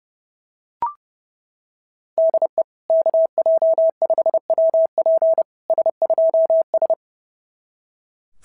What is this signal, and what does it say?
Morse code, heard as a keyed tone, about 30 wpm, 690 Hz. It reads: DE KJ5WP S2S
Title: SOTA QSO Practice 30wpm